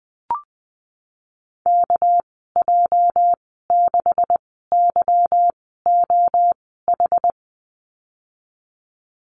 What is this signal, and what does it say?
Morse code, heard as a keyed tone, about 20 wpm, spaced as for 15 wpm, 700 Hz. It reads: KJ6YOH
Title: 238 Lesson 38g - Letter q - Call Signs 20wpm